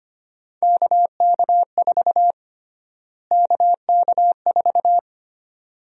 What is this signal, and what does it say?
Morse code, heard as a keyed tone, about 25 wpm, 700 Hz. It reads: KK4 KK4
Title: USA Call Sign Prefixes - Repeated 2x - 25wpm